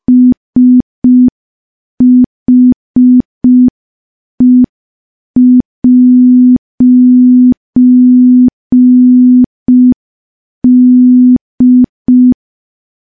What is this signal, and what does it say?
Morse code, heard as a keyed tone, about 5 wpm, 250 Hz. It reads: SHE'D
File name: morse / morse-training-05WPM